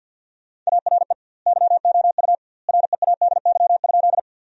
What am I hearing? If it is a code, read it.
Morse code, heard as a keyed tone, about 50 wpm, 700 Hz. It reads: ARE YOU READY?
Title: 2x Word-Spacing - Sentences from Top 400 Words - 50wpm